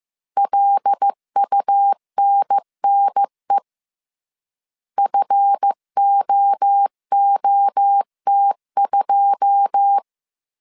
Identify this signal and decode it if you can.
Morse code, heard as a keyed tone, about 15 wpm, 800 Hz. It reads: LUNNE FOOT2
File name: CW / DET / 1 9 1500C S A 14wpm 3222